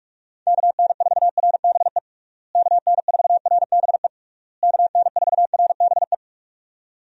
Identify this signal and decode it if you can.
Morse code, heard as a keyed tone, about 46 wpm, 700 Hz. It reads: KN4RBE KN4RBE KN4RBE
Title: USA Call Signs - 6 Characters - Repeated 3x - 45wpm